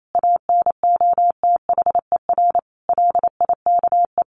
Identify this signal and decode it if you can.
Morse code, heard as a keyed tone, about 28 wpm, 700 Hz. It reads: ANOTHER LIKE
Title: A Sound Of Thunder 28wpm